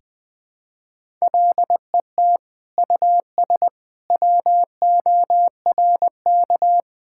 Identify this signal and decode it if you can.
Morse code, heard as a keyed tone, about 21 wpm, 700 Hz. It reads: LET US WORK
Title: Sentences from Top 1000 Words - Review Part 5 - 20wpm